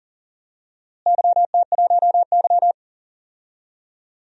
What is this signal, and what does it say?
Morse code, heard as a keyed tone, about 40 wpm, 700 Hz. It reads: YT1Y
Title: DXCC-100 Call Signs - 4 Characters - Repeated 2x - 40wpm